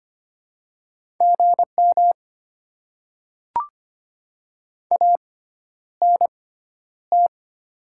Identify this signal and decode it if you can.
Morse code, heard as a keyed tone, about 25 wpm, spaced as for 11 wpm, 700 Hz. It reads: GM ANT
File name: Top 20 QSO Elements - 6x Character Spacing - 25wpm